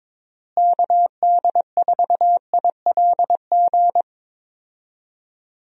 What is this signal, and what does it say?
Morse code, heard as a keyed tone, about 22 wpm, 700 Hz. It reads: KD4ILG
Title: USA Call Signs - 6 Characters - Repeated 2x - 22wpm